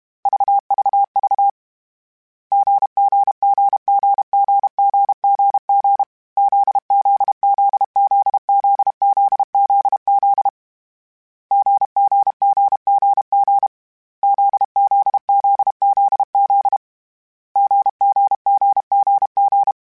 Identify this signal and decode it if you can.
Morse code, heard as a keyed tone, about 32 wpm, 800 Hz. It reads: VVV GGGGGGGG ZZZZZZZZ GGGGG ZZZZZ GGGGG
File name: Difficult Combos / Difficult Combos-mp3s / GZ GZ7 @30